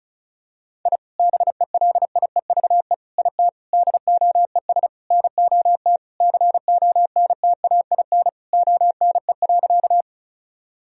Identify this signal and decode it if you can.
Morse code, heard as a keyed tone, about 36 wpm, 700 Hz. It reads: I BELIEVE IT DOES NOT CONTAIN ONE.
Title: Sentences from Top 1000 Words - Review Part 2 - 35wpm